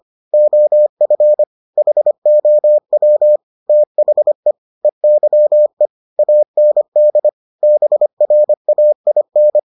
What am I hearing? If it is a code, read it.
Morse code, heard as a keyed tone, about 25 wpm, 600 Hz. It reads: OF HOW THE EYE AND BRAIN